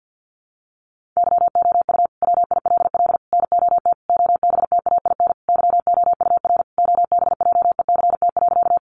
Morse code, heard as a keyed tone, about 50 wpm, 700 Hz. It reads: YOU WILL NOT OBTAIN YOUR OBJECT.
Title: Sentences from Top 1300 Words 50wpm